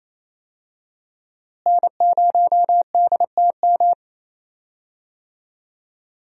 Morse code, heard as a keyed tone, about 28 wpm, 700 Hz. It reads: N0DTM
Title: USA Call Signs - 5 Characters - Repeated 3x - 28wpm